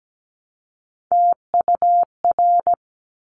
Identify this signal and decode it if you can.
Morse code, heard as a keyed tone, about 17 wpm, 700 Hz. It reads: TUR
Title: Top Three Letter Combinations in Words 17wpm